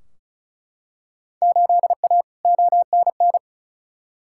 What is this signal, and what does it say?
Morse code, heard as a keyed tone, about 36 wpm, 700 Hz. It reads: 8A ONN
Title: Field Day Exchanges Encore - 35wpm